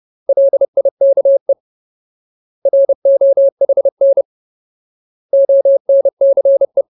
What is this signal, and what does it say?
Morse code, heard as a keyed tone, about 30 wpm, 550 Hz. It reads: LIKE ROHN ONCE
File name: Ultimate Word List / 30WPM Ultimate